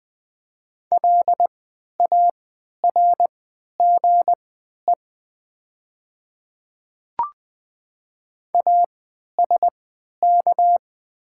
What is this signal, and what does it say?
Morse code, heard as a keyed tone, about 20 wpm, spaced as for 11 wpm, 700 Hz. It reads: LARGE ASK
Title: 3x Character Spacing - Top 200 Words - 20wpm